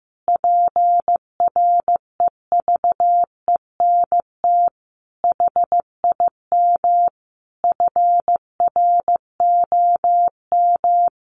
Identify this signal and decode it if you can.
Morse code, heard as a keyed tone, about 15 wpm, 700 Hz. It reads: PREVENT HIM FROM
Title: Sentences from Top 1500 Words 15wpm